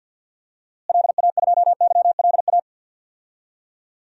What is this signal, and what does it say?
Morse code, heard as a keyed tone, about 47 wpm, 700 Hz. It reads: LA2YLA
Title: DXCC-100 Call Signs - 6 Characters - Repeated 3x - 50wpm